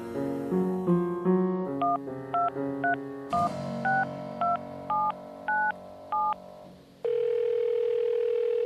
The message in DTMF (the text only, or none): 123162797